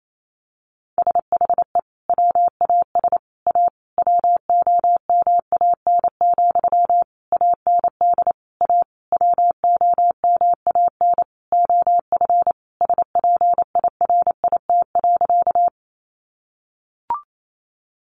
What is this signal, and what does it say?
Morse code, heard as a keyed tone, about 28 wpm, 700 Hz. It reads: SHE WAS A WOMAN, AND A WOMAN OF SPIRIT.
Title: Sentences from Top 1400 Words 28wpm